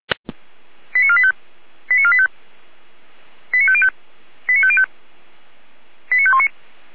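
Five-tone selective call (ZVEI), heard as quick runs of five tones, tones of about 70 ms. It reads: 89485 89485 89505 89505 89519
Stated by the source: ZVEI five-tone selective call 89020, received by SDR.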